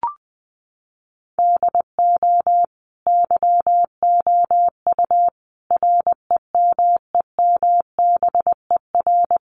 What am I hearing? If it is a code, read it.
Morse code, heard as a keyed tone, about 20 wpm, 700 Hz. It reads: DO YOU REMEMBER
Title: Sentences from Top 700 Words 20wpm